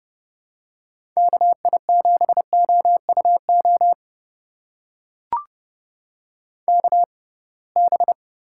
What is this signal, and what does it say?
Morse code, heard as a keyed tone, about 30 wpm, spaced as for 23 wpm, 700 Hz. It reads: KI7OUO KB
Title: USA Call Signs - 6 Characters - 6x Character Spacing - 30wpm